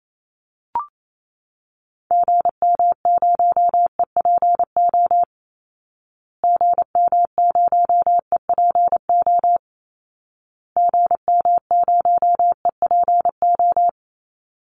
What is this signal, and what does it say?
Morse code, heard as a keyed tone, about 28 wpm, 700 Hz. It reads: GM0EPO GM0EPO GM0EPO
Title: Call Signs Repeated 2x - 28wpm